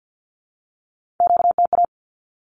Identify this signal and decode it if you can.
Morse code, heard as a keyed tone, about 50 wpm, 700 Hz. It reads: QTA
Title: Q-Codes with Spoken Meaning - 50wpm